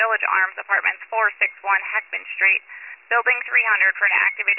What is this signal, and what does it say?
Emergency Alert System (SAME) tones, heard as harsh buzzing bursts, with speech. It is heard from 0.3 to 3.8 s.